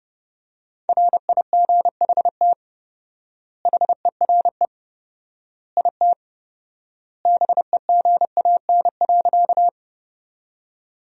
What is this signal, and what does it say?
Morse code, heard as a keyed tone, about 30 wpm, 700 Hz. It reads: RIGHT HERE IT BEGAN.
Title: 4x Word-Spacing - Sentences from Top 300 Words - 30wpm